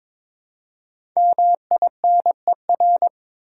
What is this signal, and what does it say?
Morse code, heard as a keyed tone, about 22 wpm, 700 Hz. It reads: MINER
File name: 5 Letter Words - First 3 Letters Given - 22wpm